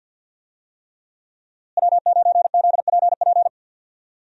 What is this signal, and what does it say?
Morse code, heard as a keyed tone, about 48 wpm, 700 Hz. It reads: W9ZPP
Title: USA Call Signs - 5 Characters - Repeated 3x - 50wpm